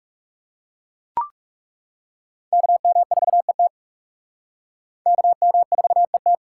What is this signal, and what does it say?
Morse code, heard as a keyed tone, about 38 wpm, 700 Hz. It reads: KM4ET KM4ET